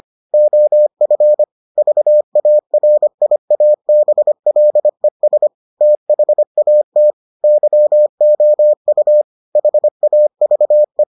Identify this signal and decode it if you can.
Morse code, heard as a keyed tone, about 25 wpm, 600 Hz. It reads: OF VARIABLES THAT YOU HAVE